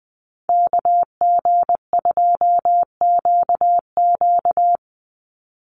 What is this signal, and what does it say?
Morse code, heard as a keyed tone, about 20 wpm, 700 Hz. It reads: KG2QQ
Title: USA Call Signs 20wpm